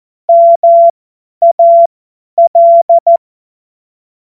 Morse code, heard as a keyed tone, about 14 wpm, spaced as for 10 wpm, 680 Hz. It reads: MAL
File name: Morsekurs / Witze / Witz 06 Februar 2024 50 F70